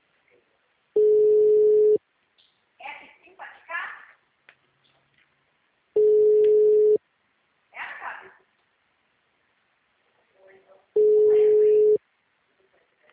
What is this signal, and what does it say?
Ringback tone (Europe, 425 Hz), with speech.